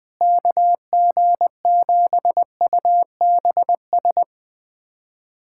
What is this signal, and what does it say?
Morse code, heard as a keyed tone, about 21 wpm, 700 Hz. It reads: KG7UBS
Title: USA Callsign DE Callsign 20wpm